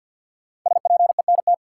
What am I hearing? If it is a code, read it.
Morse code, heard as a keyed tone, about 47 wpm, 700 Hz. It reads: SPENT
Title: Sets of 5 Words - New Word Drill - 50wpm